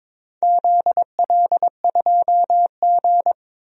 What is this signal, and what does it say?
Morse code, heard as a keyed tone, about 22 wpm, 700 Hz. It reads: ZL2G
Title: DXCC-100 Call Signs - 4 Characters - 22wpm